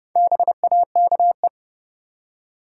Morse code, heard as a keyed tone, about 32 wpm, 700 Hz. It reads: BAKE
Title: Four Letter Words - 30wpm